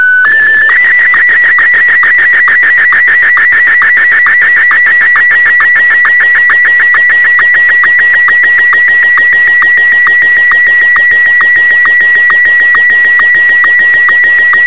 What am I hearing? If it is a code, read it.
SSTV (slow-scan television), Martin 1.